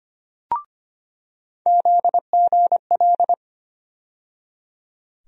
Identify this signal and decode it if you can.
Morse code, heard as a keyed tone, about 25 wpm, 700 Hz. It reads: ZGL
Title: USA Call Sign Postfixes - 3 Letters 25wpm